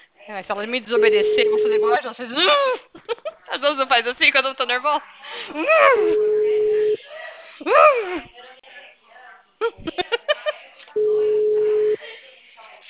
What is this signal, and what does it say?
Ringback tone (Europe, 425 Hz), with speech.
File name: out- -2004- - - .74846